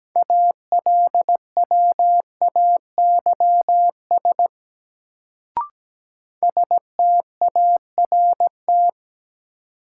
Morse code, heard as a keyed tone, about 17 wpm, 700 Hz. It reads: ALWAYS START
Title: Sentences from Top 300 Words - New Word Drill - 17wpm